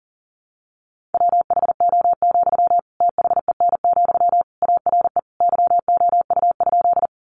Morse code, heard as a keyed tone, about 40 wpm, 700 Hz. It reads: WHO, THEN, ARE YOU?
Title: Sentences from Top 100 Words 40wpm